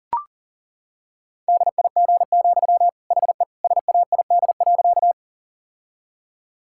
Morse code, heard as a keyed tone, about 40 wpm, 700 Hz. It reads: DIG, HE SAID.
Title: Sentences from Top 1600 Words 40wpm